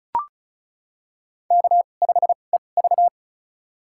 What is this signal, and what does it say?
Morse code, heard as a keyed tone, about 36 wpm, spaced as for 25 wpm, 700 Hz. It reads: K5EV
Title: USA Call Signs - 4 Characters - 2x Character Spacing - 35wpm